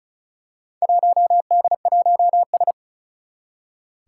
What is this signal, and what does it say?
Morse code, heard as a keyed tone, about 36 wpm, 700 Hz. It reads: 1D1S
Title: Four Challenging Letter-Number Combinations 35wpm